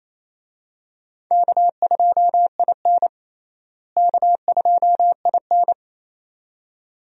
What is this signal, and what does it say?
Morse code, heard as a keyed tone, about 27 wpm, 700 Hz. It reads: K2IN K2IN
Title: USA Call Signs - 4 Characters - Repeated 2x - 28wpm